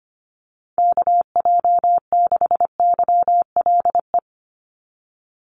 Morse code, heard as a keyed tone, about 25 wpm, 700 Hz. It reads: KJ6YLE
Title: USA Call Signs 25wpm